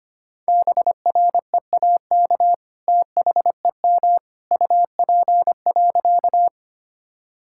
Morse code, heard as a keyed tone, about 25 wpm, 700 Hz. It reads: BREAK THEM UP.
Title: Sentences from Top 1000 Words - Review Part 6 - 25wpm